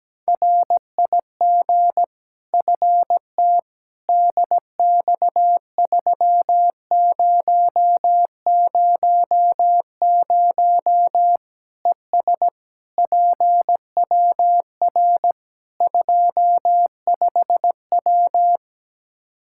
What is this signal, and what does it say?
Morse code, heard as a keyed tone, about 17 wpm, 700 Hz. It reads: RIG FT DX3000 ES PWR 25W
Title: K1USN SST QSO Practice 17wpm